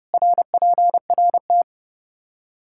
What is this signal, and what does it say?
Morse code, heard as a keyed tone, about 30 wpm, 700 Hz. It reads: RPRT
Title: Practice QSO - Exchange 2 - Info - 30wpm